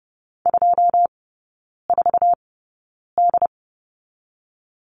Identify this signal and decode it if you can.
Morse code, heard as a keyed tone, about 30 wpm, spaced as for 9 wpm, 690 Hz. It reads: 24D
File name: Lesson 17e - Letter u - Three Characters 30wpm